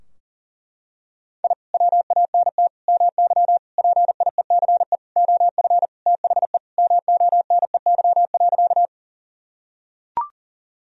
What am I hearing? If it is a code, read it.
Morse code, heard as a keyed tone, about 40 wpm, 700 Hz. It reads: I WANT MY PIECE OF THE MONEY.
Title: Sentences from Top 1000 Words - Review Part 2 - 40wpm